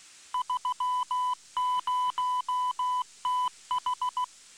Morse code, heard as a keyed tone, about 16 wpm, 1000 Hz. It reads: 30TH